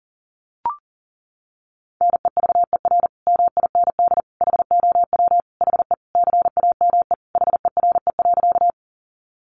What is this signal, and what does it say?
Morse code, heard as a keyed tone, about 37 wpm, 700 Hz. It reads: NEVER MIND HOW HE CAME HERE.
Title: Sentences from Top 500 Words 40wpm